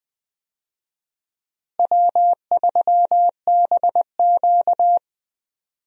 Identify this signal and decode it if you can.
Morse code, heard as a keyed tone, about 21 wpm, 700 Hz. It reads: W3BQ
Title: USA Call Signs - 4 Characters - Repeated 2x - 20wpm